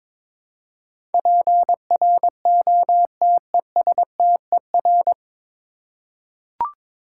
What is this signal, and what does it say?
Morse code, heard as a keyed tone, about 21 wpm, 700 Hz. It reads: PROTESTER
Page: Morse Code Ninja practice 22wpm